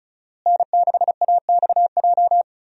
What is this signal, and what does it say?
Morse code, heard as a keyed tone, about 36 wpm, 700 Hz. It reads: N6AXJ
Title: USA Call Signs - 5 Characters - Repeated 2x - 35wpm